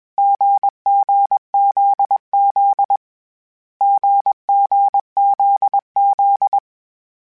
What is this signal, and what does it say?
Morse code, heard as a keyed tone, about 21 wpm, 800 Hz. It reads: GGZZ GGZZ
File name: Short Runs / Short Runs-mp3s / GZ GZ7 @20